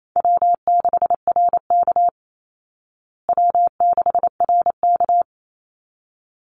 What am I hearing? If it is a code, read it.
Morse code, heard as a keyed tone, about 28 wpm, 700 Hz. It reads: W6RK W6RK
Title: Easy Call Signs Repeated 2x - 28wpm